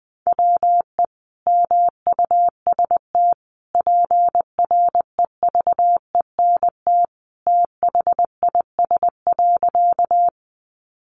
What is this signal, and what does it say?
Morse code, heard as a keyed tone, about 19 wpm, 700 Hz. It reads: WE MUST PREVENT THIS.
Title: Sentences from Top 1100 Words 20wpm